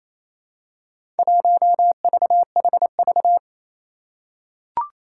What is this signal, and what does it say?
Morse code, heard as a keyed tone, about 28 wpm, 700 Hz. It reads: 1VHV
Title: Four Challenging Letter-Number Combinations 28wpm